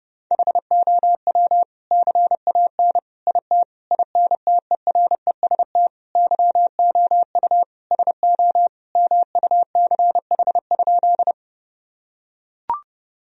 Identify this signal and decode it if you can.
Morse code, heard as a keyed tone, about 30 wpm, 700 Hz. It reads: HOW CAN IT INTEREST YOU SO MUCH?
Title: Sentences from Top 1000 Words - Review Part 2 - 30wpm